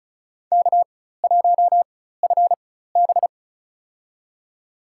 Morse code, heard as a keyed tone, about 34 wpm, spaced as for 16 wpm, 700 Hz. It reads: K1FB
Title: USA Call Signs - 4 Characters - 4x Character Spacing - 35wpm